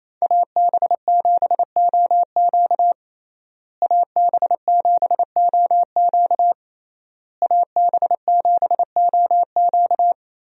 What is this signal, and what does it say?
Morse code, heard as a keyed tone, about 28 wpm, 700 Hz. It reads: AB7OQ AB7OQ AB7OQ
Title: USA Call Signs - 5 Characters - Repeated 3x - 28wpm